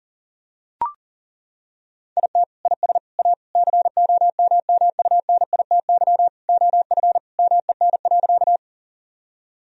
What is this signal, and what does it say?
Morse code, heard as a keyed tone, about 40 wpm, 700 Hz. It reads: IT IS A COMMUNITY OF MEN.
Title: Sentences from Top 1000 Words - Review Part 4 - 40wpm